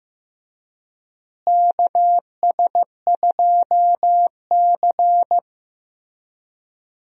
Morse code, heard as a keyed tone, about 15 wpm, 700 Hz. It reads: KS2C
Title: USA Call Signs - 4 Characters - Repeated 2x - 15wpm